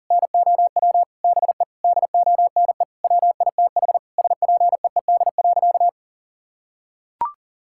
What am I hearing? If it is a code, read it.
Morse code, heard as a keyed tone, about 40 wpm, 700 Hz. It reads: NOW BE DONE WITH SPEED.
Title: Sentences from Top 1000 Words - Review Part 5 - 40wpm